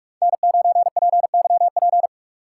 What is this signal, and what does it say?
Morse code, heard as a keyed tone, about 59 wpm, spaced as for 51 wpm, 700 Hz. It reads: N9PYP
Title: Speed Racing - USA 1x3 Call Signs 45wpm